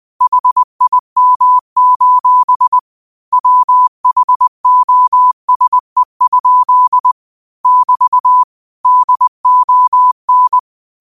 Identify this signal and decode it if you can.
Morse code, heard as a keyed tone, about 20 wpm, 1000 Hz. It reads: HIM: WHOSE? <BT> DON
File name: qotdfiles / 20wpm